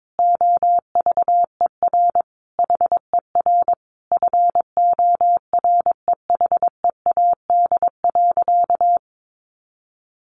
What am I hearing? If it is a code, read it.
Morse code, heard as a keyed tone, about 22 wpm, 700 Hz. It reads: OVER HER FOREHEAD.
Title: Easy Phrases 22wpm